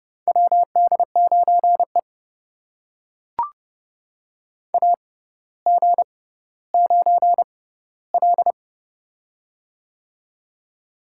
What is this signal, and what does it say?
Morse code, heard as a keyed tone, about 30 wpm, spaced as for 15 wpm, 700 Hz. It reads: WD9E AG9L
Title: USA Call Signs - 4 Characters - 6x Character Spacing - 30wpm